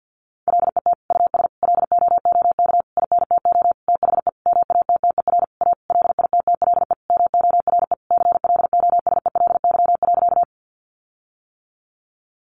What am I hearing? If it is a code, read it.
Morse code, heard as a keyed tone, about 50 wpm, 700 Hz. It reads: LET US LOOK INTO THE MATTER A LITTLE MORE CLOSELY.
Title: Sentences from Top 1400 Words 50wpm